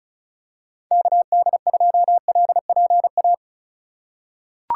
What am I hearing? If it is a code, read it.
Morse code, heard as a keyed tone, about 36 wpm, 700 Hz. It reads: KD2LPA
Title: USA Call Signs - 6 Characters - 4x Character Spacing - 35wpm